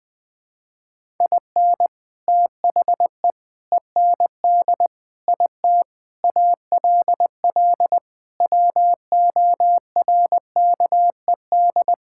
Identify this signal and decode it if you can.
Morse code, heard as a keyed tone, about 19 wpm, 700 Hz. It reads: IN THE END IT ALL WORKED
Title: Sentences with 3-Word Phrases - Spoken Beginning - 20wpm